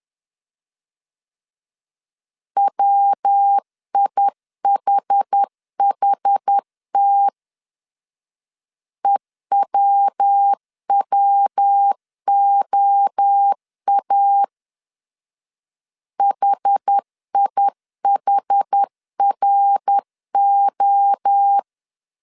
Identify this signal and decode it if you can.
Morse code, heard as a keyed tone, about 10 wpm, 800 Hz. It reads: WIHHT EWWOA HIHRO
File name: CW / DET / 1 3 1500C S B 10wpm 3335